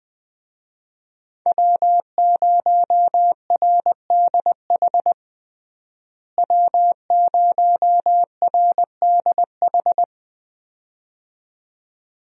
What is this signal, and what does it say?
Morse code, heard as a keyed tone, about 20 wpm, 700 Hz. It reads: W0RDH W0RDH
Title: USA Call Signs - 5 Characters - Repeated 2x - 20wpm